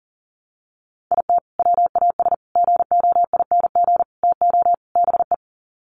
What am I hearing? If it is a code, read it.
Morse code, heard as a keyed tone, about 40 wpm, 700 Hz. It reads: IT WAS GOING TO BE
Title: Sets of 5 Words 40wpm